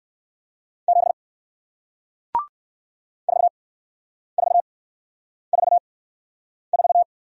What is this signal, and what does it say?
Morse code, heard as a keyed tone, about 57 wpm, spaced as for 9 wpm, 700 Hz. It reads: 7 4444